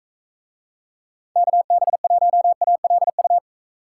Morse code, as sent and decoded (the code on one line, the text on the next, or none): -.- -... .---- .- .-.. ..-
KB1ALU